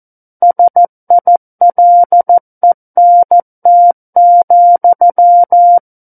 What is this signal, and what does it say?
Morse code, heard as a keyed tone, about 14 wpm, 700 Hz. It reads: SILENT,
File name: Book One / 05-Chapter 5 - 14 WPM